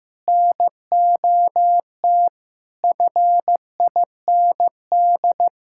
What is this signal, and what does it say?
Morse code, heard as a keyed tone, about 15 wpm, 700 Hz. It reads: NOT FIND
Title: Sentences from Top 1000 Words - Review Part 5 - 15wpm